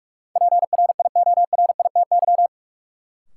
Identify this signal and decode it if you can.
Morse code, heard as a keyed tone, about 46 wpm, 700 Hz. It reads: PRIORITY
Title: Sentences from Top 1600 Words - New Word Drill - 45wpm